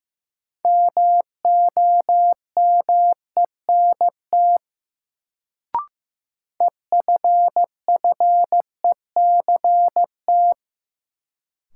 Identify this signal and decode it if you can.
Morse code, heard as a keyed tone, about 15 wpm, 700 Hz. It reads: MOMENT EFFECT
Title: Sentences from Top 1000 Words - New Word Drill - 15wpm